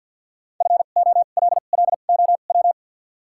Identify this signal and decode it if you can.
Morse code, heard as a keyed tone, about 53 wpm, spaced as for 37 wpm, 700 Hz. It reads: FOLLOW
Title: 2x Character Spacing - Top 300 Words - 50wpm